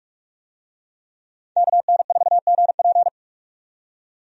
Morse code, heard as a keyed tone, about 46 wpm, 700 Hz. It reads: KN4GP
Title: USA Call Signs - 5 Characters - 45wpm